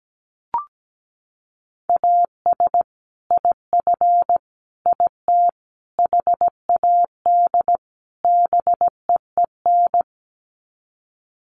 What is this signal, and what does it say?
Morse code, heard as a keyed tone, about 17 wpm, 700 Hz. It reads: AS IF IT HAD BEEN
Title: Sets of 5 Words Encore2 17wpm